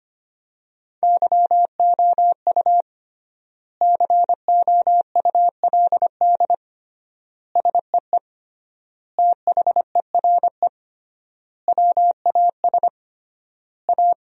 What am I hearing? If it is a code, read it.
Morse code, heard as a keyed tone, about 25 wpm, 700 Hz. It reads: YOU COULD SEE THERE WAS A
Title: 3x Word-Spacing - Sentences from Top 500 Words - 25wpm